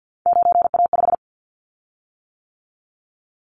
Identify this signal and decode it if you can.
Morse code, heard as a keyed tone, about 50 wpm, 700 Hz. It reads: 9A5
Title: Callsign Prefixes for Top 50 DXCC - 50wpm